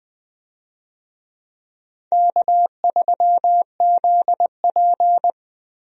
Morse code, heard as a keyed tone, about 20 wpm, 700 Hz. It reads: K3ZP
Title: USA Call Signs - 4 Characters - Repeated 2x - 20wpm